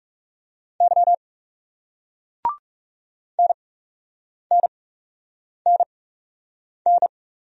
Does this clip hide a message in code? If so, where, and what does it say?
Yes: Morse code from 0.8 to 7.1 s, heard as a keyed tone, about 40 wpm, spaced as for 8 wpm, 700 Hz. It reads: Y NNNN